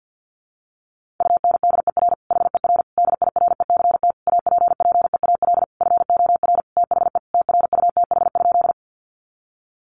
Morse code, heard as a keyed tone, about 50 wpm, 700 Hz. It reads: UNDER HER DIRECT APPEAL FOR THE TRUTH?
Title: Sentences from Top 1300 Words 50wpm